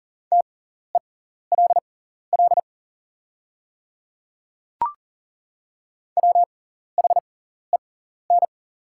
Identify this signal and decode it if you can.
Morse code, heard as a keyed tone, about 41 wpm, spaced as for 14 wpm, 690 Hz. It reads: TELL WHEN
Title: 6x Character Spacing - Top 300 Words - 40wpm